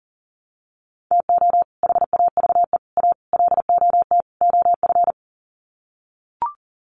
Morse code, heard as a keyed tone, about 37 wpm, 700 Hz. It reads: TO HAVE A LOT OF
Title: Sets of 5 Words Encore 40wpm